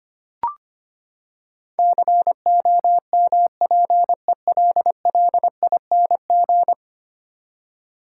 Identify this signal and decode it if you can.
Morse code, heard as a keyed tone, about 25 wpm, 700 Hz. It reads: COMPELLING
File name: Ten Letter Words - 25wpm